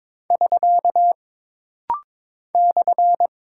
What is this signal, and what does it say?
Morse code, heard as a keyed tone, about 22 wpm, 700 Hz. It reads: <SK> /
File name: practice / 22wpm / Single Punctuation-Prosign - Rapid-Fire - 22wpm